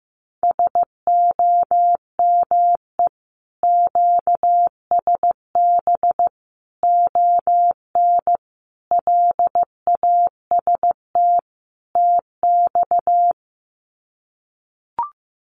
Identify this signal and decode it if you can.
Morse code, heard as a keyed tone, about 15 wpm, 700 Hz. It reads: SOME QSB ON LAST TX
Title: Common QSO Phrases 15wpm